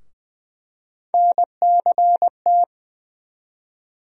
Morse code, heard as a keyed tone, about 20 wpm, 700 Hz. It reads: NCT
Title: USA Call Sign Postfixes - 3 Letters 20wpm